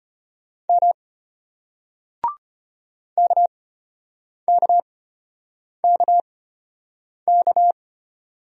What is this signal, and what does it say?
Morse code, heard as a keyed tone, about 34 wpm, spaced as for 8 wpm, 700 Hz. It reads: M KKKK